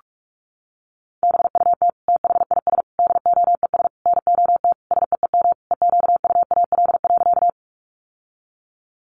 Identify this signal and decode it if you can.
Morse code, heard as a keyed tone, about 44 wpm, 700 Hz. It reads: BUT THIS DOES NOT SEEM EQUAL.
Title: Sentences from Top 1400 Words 45wpm